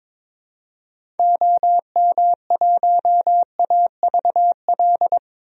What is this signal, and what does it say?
Morse code, heard as a keyed tone, about 22 wpm, 700 Hz. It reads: OM1AVL
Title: DXCC-100 Call Signs - 6 Characters - 22wpm